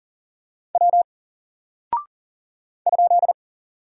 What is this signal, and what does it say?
Morse code, heard as a keyed tone, about 40 wpm, 700 Hz. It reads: W ?
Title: Single Character (Letter:Number:Punctuation:Prosign) 40wpm